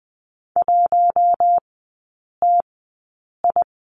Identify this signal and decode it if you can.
Morse code, heard as a keyed tone, about 20 wpm, spaced as for 8 wpm, 700 Hz. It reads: 1TI
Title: 041 Lesson 8e - Number 1 - Three Characters 20wpm